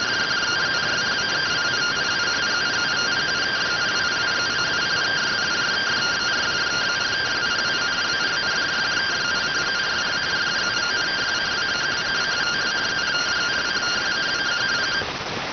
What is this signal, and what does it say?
Two-tone teleprinter signal, about 170 Hz shift.